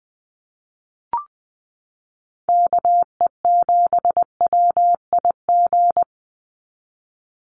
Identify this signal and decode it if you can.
Morse code, heard as a keyed tone, about 20 wpm, 700 Hz. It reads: KE7WIG
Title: USA Call Signs 20wpm